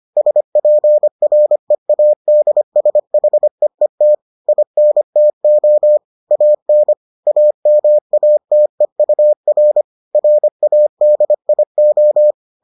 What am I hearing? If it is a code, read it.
Morse code, heard as a keyed tone, about 25 wpm, 600 Hz. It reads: SPREADSHEET INTO AN AMATEUR RADIO